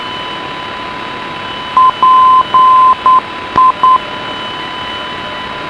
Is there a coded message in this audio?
Yes: Morse code from 1.8 to 4.0 s, heard as a keyed tone, about 9 wpm, 1030 Hz, with radio noise. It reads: PI